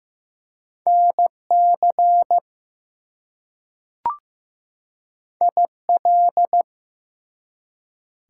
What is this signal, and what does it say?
Morse code, heard as a keyed tone, about 15 wpm, 700 Hz. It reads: NC IL
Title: US States and Canadian Provinces 15wpm